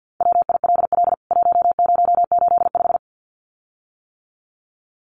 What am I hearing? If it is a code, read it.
Morse code, heard as a keyed tone, about 50 wpm, 700 Hz. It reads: WILL 1085
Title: CWT Exchange 50wpm